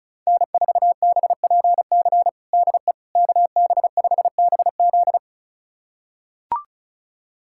Morse code, heard as a keyed tone, about 34 wpm, 700 Hz. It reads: N4BPC DE KB5BZ
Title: USA Callsign DE Callsign - 5 Characters - 35wpm